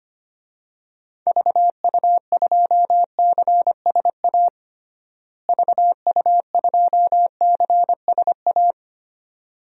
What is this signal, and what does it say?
Morse code, heard as a keyed tone, about 25 wpm, 700 Hz. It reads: VU2CSA VU2CSA
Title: DXCC-100 Call Signs - 6 Characters - Repeated 2x - 25wpm